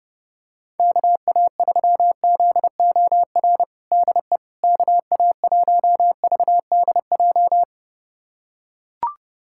Morse code, heard as a keyed tone, about 30 wpm, 700 Hz. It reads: KA3ZOR DE KA1VDJ
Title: USA Callsign DE Callsign - 6 Characters - 30wpm